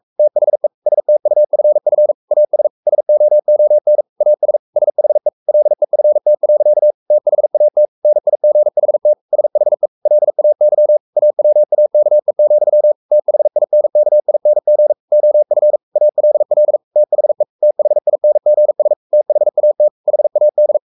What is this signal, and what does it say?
Morse code, heard as a keyed tone, about 44 wpm, 600 Hz. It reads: THE STUFF AS SOON AS SHE LEFT. THAT NIGHT SHE LAY AWAKE, THINKING OF ALL THE THINGS THAT HAD
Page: Romanen Never af Ken Follet på engelsk.